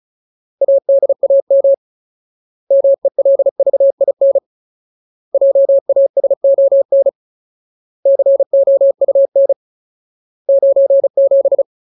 Morse code, heard as a keyed tone, about 34 wpm, 550 Hz. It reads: ADAM MELVIN JASON COUN 97
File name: Ultimate Word List / 35WPM Ultimate